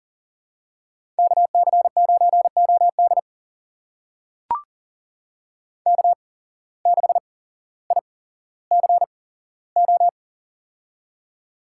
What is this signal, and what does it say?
Morse code, heard as a keyed tone, about 40 wpm, spaced as for 17 wpm, 700 Hz. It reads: KC9OD K6ICO
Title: USA Call Signs - 5 Characters - 8x Character Spacing - 40wpm